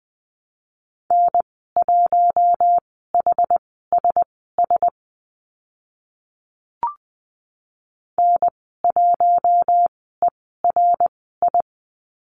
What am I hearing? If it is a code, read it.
Morse code, heard as a keyed tone, about 20 wpm, spaced as for 14 wpm, 700 Hz. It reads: N1HSS N1ERI
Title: 076 Lesson 13g - Letter L - Call Signs 20wpm